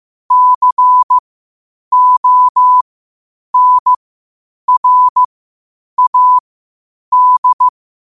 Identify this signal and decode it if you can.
Morse code, heard as a keyed tone, about 15 wpm, spaced as for 8 wpm, 1000 Hz. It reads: CONRAD
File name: qotdfiles / 8wpm